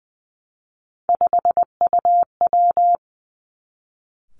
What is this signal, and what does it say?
Morse code, heard as a keyed tone, about 20 wpm, 700 Hz. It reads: 5UW
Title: Three Challenging Letter-Number Combinations 20wpm